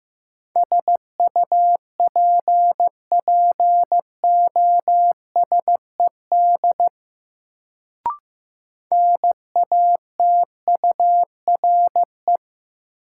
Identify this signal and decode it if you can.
Morse code, heard as a keyed tone, about 15 wpm, 700 Hz. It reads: SUPPOSED NATURE
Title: Sets of 3 Words - New Word Drill - 15wpm